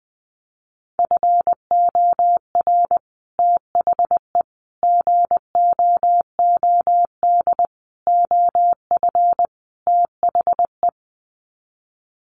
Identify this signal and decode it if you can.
Morse code, heard as a keyed tone, about 20 wpm, 700 Hz. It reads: FOR THE GOOD OF THE
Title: Sets of 5 Words Encore2 20wpm